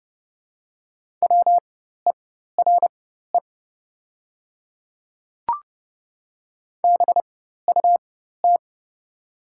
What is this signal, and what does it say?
Morse code, heard as a keyed tone, about 31 wpm, spaced as for 14 wpm, 690 Hz. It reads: WERE BUT
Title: 4x Character Spacing - Top 100 Words - 30wpm